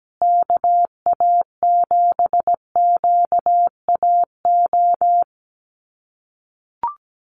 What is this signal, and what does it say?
Morse code, heard as a keyed tone, about 17 wpm, 700 Hz. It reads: KA7QAO
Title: USA Call Signs Repeated 2x 17wpm